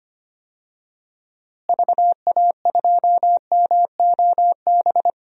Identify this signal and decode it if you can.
Morse code, heard as a keyed tone, about 25 wpm, 700 Hz. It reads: VA2MOB
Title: DXCC-100 Call Signs - 6 Characters - Repeated 3x - 25wpm